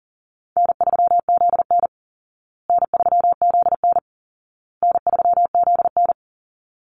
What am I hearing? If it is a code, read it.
Morse code, heard as a keyed tone, about 37 wpm, 700 Hz. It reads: N3ZN N3ZN N3ZN
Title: Easy Call Signs Repeated 3x - 40wpm